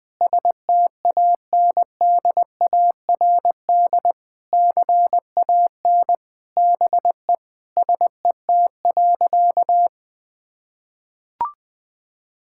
Morse code, heard as a keyed tone, about 21 wpm, 700 Hz. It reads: STANDARD CAN BE SET.
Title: Sentences from Top 1000 Words - Review Part 6 - 20wpm